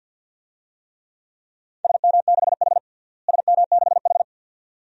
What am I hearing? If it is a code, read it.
Morse code, heard as a keyed tone, about 53 wpm, 700 Hz. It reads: SM6H SM6H
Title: DXCC-100 Call Signs - 4 Characters - Repeated 2x - 50wpm